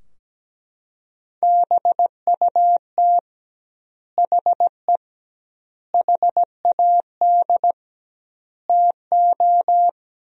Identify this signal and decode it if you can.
Morse code, heard as a keyed tone, about 17 wpm, 700 Hz. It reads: BUT HE HAD TO
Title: 2x Word-Spacing - Sentences from Top 100 Words - 17wpm